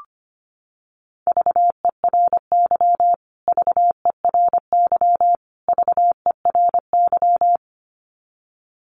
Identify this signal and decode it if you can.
Morse code, heard as a keyed tone, about 25 wpm, 700 Hz. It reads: VERY VERY VERY
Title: Top 200 Words Repeated 25wpm